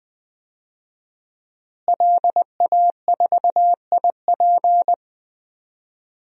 Morse code, heard as a keyed tone, about 20 wpm, 700 Hz. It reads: LA4IP
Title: DXCC-100 Call Signs - 5 Characters - Repeated 2x - 20wpm